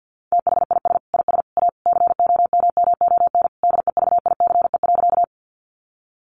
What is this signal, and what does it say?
Morse code, heard as a keyed tone, about 50 wpm, 700 Hz. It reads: THIS IS A COMMON DEVICE.
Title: Sentences from Top 1500 Words 50wpm